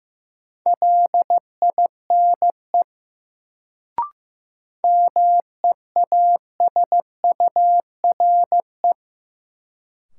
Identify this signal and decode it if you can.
Morse code, heard as a keyed tone, about 15 wpm, 700 Hz. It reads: LINE MEASURE
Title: Sentences from Top 1200 Words - New Word Drill Alt - 15wpm